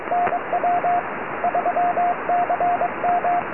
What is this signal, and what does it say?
Morse code, heard as a keyed tone, about 21 wpm, 710 Hz, with radio noise. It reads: NW 3CM
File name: tropo / sp6asd 23cm